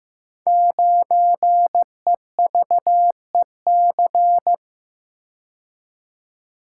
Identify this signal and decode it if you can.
Morse code, heard as a keyed tone, about 15 wpm, 700 Hz. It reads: 9EVEC
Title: Code Groups - 5 Letter:Numbers - 15wpm